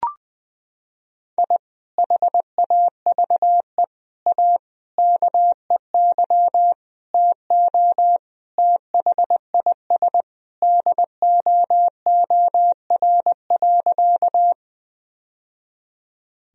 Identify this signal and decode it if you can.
Morse code, heard as a keyed tone, about 21 wpm, 700 Hz. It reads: I HAVE A KEY TO THIS DOOR.
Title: Sentences from Top 1000 Words - Review Part 4 - 20wpm